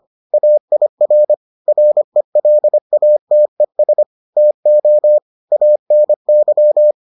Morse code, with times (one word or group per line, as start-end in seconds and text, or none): 0.3-1.3 AIR
1.7-4.0 RELATES
4.4-5.2 TO
5.5-6.9 ANY